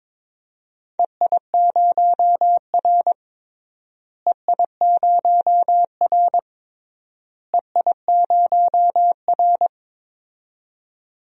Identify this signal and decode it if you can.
Morse code, heard as a keyed tone, about 21 wpm, 700 Hz. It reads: EI0R EI0R EI0R
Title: DXCC-100 Call Signs - 4 Characters - Repeated 3x - 22wpm